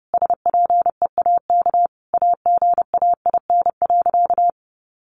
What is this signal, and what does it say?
Morse code, heard as a keyed tone, about 30 wpm, 700 Hz. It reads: SPEAK AGAIN.
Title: Sentences from Top 1500 Words 30wpm